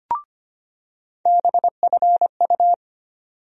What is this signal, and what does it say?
Morse code, heard as a keyed tone, about 25 wpm, 700 Hz. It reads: BFU
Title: USA Call Sign Postfixes - 3 Letters 25wpm